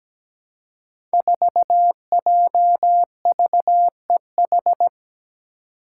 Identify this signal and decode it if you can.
Morse code, heard as a keyed tone, about 17 wpm, 700 Hz. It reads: 4JVEH
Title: Five Challenging Letter-Number Combinations 17wpm